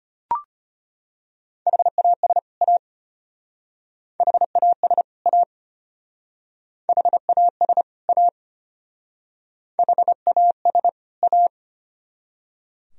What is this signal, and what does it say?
Morse code, heard as a keyed tone, about 31 wpm, 700 Hz. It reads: HAS A HAS A HAS A HAS A
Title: Speed-Racing - Sets of 2 Words 25wpm